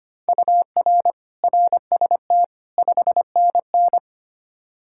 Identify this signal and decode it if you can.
Morse code, heard as a keyed tone, about 25 wpm, 700 Hz. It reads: UR RST 5NN
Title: Practice QSO - Exchange 1 - Beginning - 25wpm